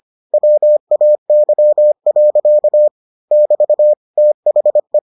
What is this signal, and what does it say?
Morse code, heard as a keyed tone, about 25 wpm, 600 Hz. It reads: WAY. <BT> THE